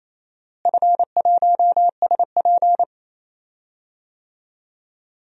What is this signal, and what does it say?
Morse code, heard as a keyed tone, about 28 wpm, 700 Hz. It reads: F1SP
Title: DXCC-100 Call Signs - 4 Characters - Repeated 2x - 28wpm